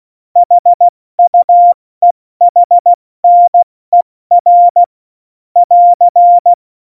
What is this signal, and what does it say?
Morse code, heard as a keyed tone, about 16 wpm, 700 Hz. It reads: HUEHNER <AR>
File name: Morsekurs / Witze / Witz 01. August 17, Tempo 70 F80